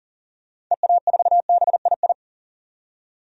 Morse code, heard as a keyed tone, about 40 wpm, 700 Hz. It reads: EA4BII
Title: DXCC-100 Call Signs - 6 Characters - 40wpm